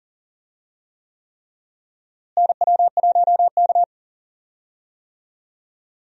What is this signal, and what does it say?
Morse code, heard as a keyed tone, about 40 wpm, 700 Hz. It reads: NW1K
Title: USA Call Signs - 4 Characters - Repeated 3x - 40wpm